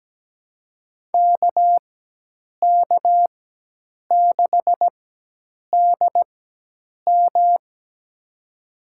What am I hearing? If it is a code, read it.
Morse code, heard as a keyed tone, about 17 wpm, spaced as for 8 wpm, 700 Hz. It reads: KK6DM
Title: USA Call Signs - 4x Character Spacing - 17wpm